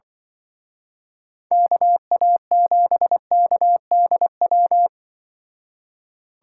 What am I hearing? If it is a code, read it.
Morse code, heard as a keyed tone, about 24 wpm, 700 Hz. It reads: KA7KDW
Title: Speed Racing - USA 2x3 Call Signs 20wpm